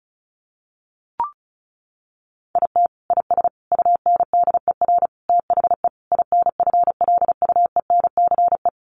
Morse code, heard as a keyed tone, about 34 wpm, 700 Hz. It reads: IT IS UNDER THE INFLUENCE
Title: Sentences from Top 1400 Words 35wpm